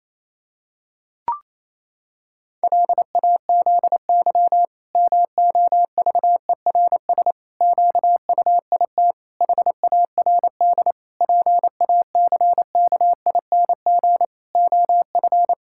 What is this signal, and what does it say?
Morse code, heard as a keyed tone, about 28 wpm, 700 Hz. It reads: LAZY MOVERS QUIT HARD PACKING OF
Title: Pangram Challenge v2 - 28wpm